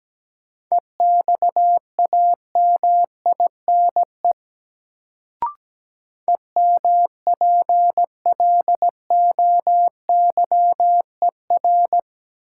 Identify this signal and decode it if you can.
Morse code, heard as a keyed tone, about 17 wpm, 700 Hz. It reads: EXAMINE EMPLOYER
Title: Sentences from Top 1600 Words - New Word Drill Alt - 17wpm